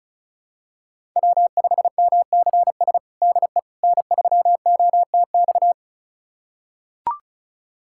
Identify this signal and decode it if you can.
Morse code, heard as a keyed tone, about 36 wpm, 700 Hz. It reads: W5MCS DE N3OTX
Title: USA Callsign DE Callsign - 5 Characters - 35wpm